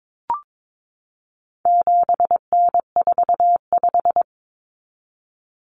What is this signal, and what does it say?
Morse code, heard as a keyed tone, about 22 wpm, 700 Hz. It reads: 7N45
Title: Common Cut Numbers - 4-Digit 22wpm